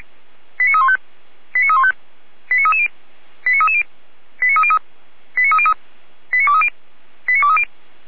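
Five-tone selective call (ZVEI), heard as quick runs of five tones, tones of about 70 ms. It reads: 89315 89315 89330 89330 89303 89303 89230 89230
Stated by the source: Five-tone ZVEI selective call received by SDR.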